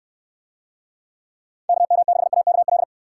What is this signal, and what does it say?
Morse code, heard as a keyed tone, about 69 wpm, 700 Hz. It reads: KM6AKL